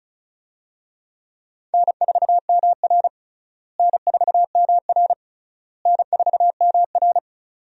Morse code, heard as a keyed tone, about 36 wpm, 700 Hz. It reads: N4MR N4MR N4MR
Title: USA Call Signs - 4 Characters - Repeated 3x - 35wpm